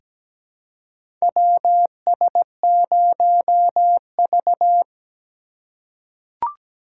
Morse code, heard as a keyed tone, about 17 wpm, 700 Hz. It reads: WS0V
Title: USA Call Signs - 4 Characters - 4x Character Spacing - 17wpm